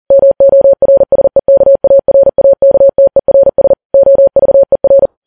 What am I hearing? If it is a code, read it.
Morse code, heard as a keyed tone, about 40 wpm, 550 Hz. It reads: MORSEKARAKTERS OVER
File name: Files Web tls / Morse on9fv / morse40kps